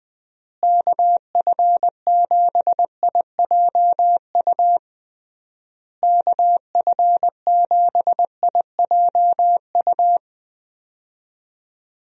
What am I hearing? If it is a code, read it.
Morse code, heard as a keyed tone, about 20 wpm, 700 Hz. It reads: KF7IJU KF7IJU
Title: USA Call Signs - 6 Characters - Repeated 3x - 20wpm